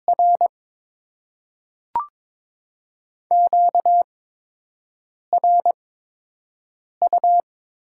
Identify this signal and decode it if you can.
Morse code, heard as a keyed tone, about 22 wpm, spaced as for 6 wpm, 700 Hz. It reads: R QRU
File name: Top 90 QSO Elements - 8x Character Spacing - 22wpm